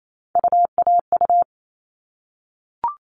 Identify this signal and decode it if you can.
Morse code, heard as a keyed tone, about 28 wpm, 700 Hz. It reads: UAU
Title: All Cut Numbers - 3-Digit 28wpm